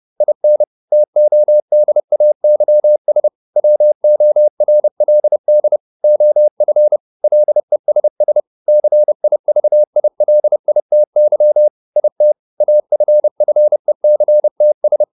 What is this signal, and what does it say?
Morse code, heard as a keyed tone, about 30 wpm, 600 Hz. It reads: IN TODAYS WORLD OF LESS CIVILITY IT AFFECTS